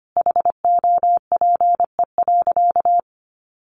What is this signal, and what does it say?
Morse code, heard as a keyed tone, about 25 wpm, 700 Hz. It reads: HOPE.
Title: Sentences from Top 1000 Words 25wpm